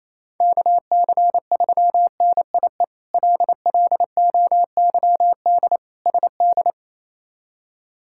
Morse code, heard as a keyed tone, about 27 wpm, 700 Hz. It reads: KC3NIE LLOYD SD
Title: K1USN SST QSO Practice - 28wpm